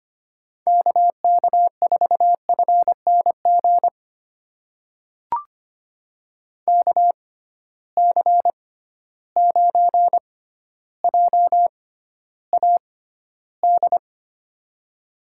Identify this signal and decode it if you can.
Morse code, heard as a keyed tone, about 25 wpm, spaced as for 13 wpm, 700 Hz. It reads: KK4FNG KC9JAD
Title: USA Call Signs - 6 Characters - 6x Character Spacing - 25wpm